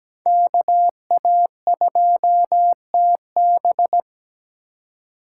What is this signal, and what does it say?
Morse code, heard as a keyed tone, about 17 wpm, 700 Hz. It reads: KA2TB
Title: Code Groups - 5 Letter:Numbers - 17wpm